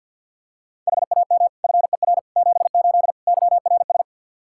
Morse code, heard as a keyed tone, about 51 wpm, 700 Hz. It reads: HAM FER 78 YRS
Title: Practice QSO - Exchange 3 - More Info - 50wpm